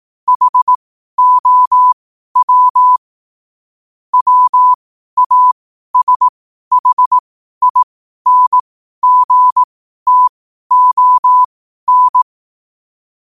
Morse code, heard as a keyed tone, about 18 wpm, spaced as for 12 wpm, 1000 Hz. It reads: HOW WASHINGTON
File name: qotdfiles / 12wpm